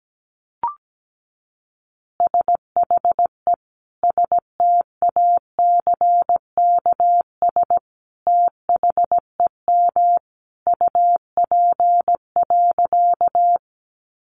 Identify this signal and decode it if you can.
Morse code, heard as a keyed tone, about 17 wpm, 700 Hz. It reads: SHE STACKS THEM UP.
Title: Easy Phrases 17wpm